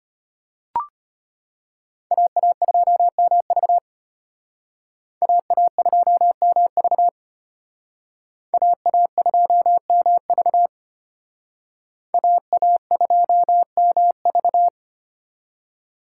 Morse code, heard as a keyed tone, about 33 wpm, 700 Hz. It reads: AA2MV AA2MV AA2MV AA2MV